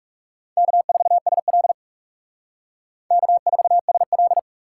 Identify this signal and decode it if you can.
Morse code, heard as a keyed tone, about 36 wpm, 700 Hz. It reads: K4SL K4SL
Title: Speed Racing - USA 1x2 Call Signs 30wpm